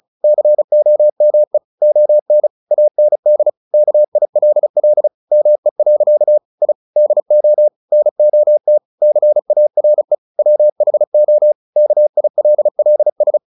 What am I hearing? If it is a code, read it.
Morse code, heard as a keyed tone, about 34 wpm, 600 Hz. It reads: COME ON AND KILL ME. I DO NOT CARE WHO KILLS
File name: the old man and the sea 2 35wpm